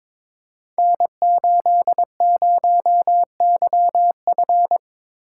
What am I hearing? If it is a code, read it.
Morse code, heard as a keyed tone, about 22 wpm, 700 Hz. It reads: N80YF
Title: Code Groups - 5 Letter:Numbers - 22wpm